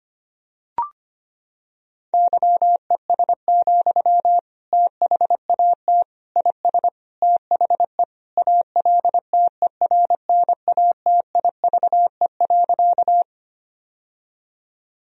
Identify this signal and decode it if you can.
Morse code, heard as a keyed tone, about 25 wpm, 700 Hz. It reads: YES, THAT IS THE ALTERNATIVE.
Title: Sentences from Top 1000 Words - Review Part 4 - 25wpm